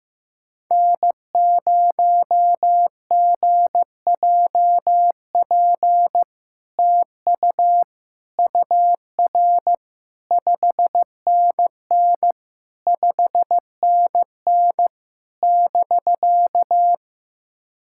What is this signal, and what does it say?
Morse code, heard as a keyed tone, about 15 wpm, 700 Hz. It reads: N0GJP TU UR 5NN 5NN <BK>
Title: SOTA QSO Practice 15wpm